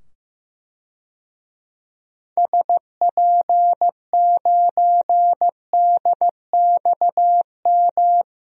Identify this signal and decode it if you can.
Morse code, heard as a keyed tone, about 15 wpm, 700 Hz. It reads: SP9DXM
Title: DXCC-100 Call Signs - 6 Characters - 15wpm